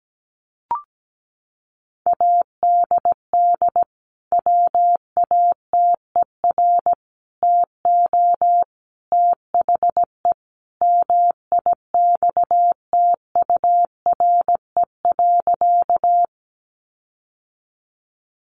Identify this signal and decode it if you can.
Morse code, heard as a keyed tone, about 17 wpm, 700 Hz. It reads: ADD WATER TO THE MIXTURE.
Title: Sentences from Top 1500 Words 17wpm